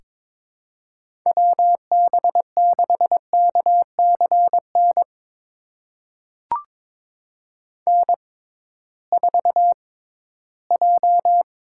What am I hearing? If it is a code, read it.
Morse code, heard as a keyed tone, about 22 wpm, spaced as for 9 wpm, 700 Hz. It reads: WB6KC NN4J
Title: USA Call Signs - 6x Character Spacing - 22wpm